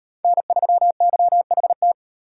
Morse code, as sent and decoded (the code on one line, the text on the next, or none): -. ...-- -.-- .... -
N3YHT